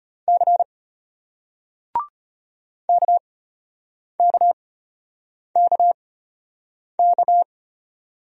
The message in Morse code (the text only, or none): C KKKK